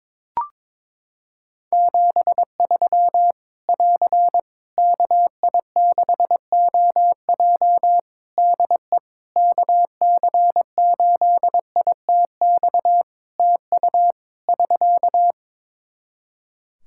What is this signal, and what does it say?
Morse code, heard as a keyed tone, about 22 wpm, 700 Hz. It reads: 73 <AR> KI6OJ DE KC8ITX TU <SK>
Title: K1USN SST QSO Practice 22wpm